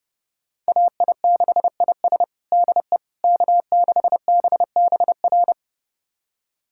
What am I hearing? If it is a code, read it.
Morse code, heard as a keyed tone, about 32 wpm, 700 Hz. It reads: AI6IS DE K6BBR
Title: USA Callsign DE Callsign - 5 Characters - 30wpm